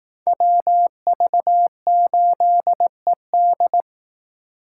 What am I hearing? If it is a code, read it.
Morse code, heard as a keyed tone, about 18 wpm, 700 Hz. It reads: WV8ED
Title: Speed Racing - USA 2x2 Call Signs 15wpm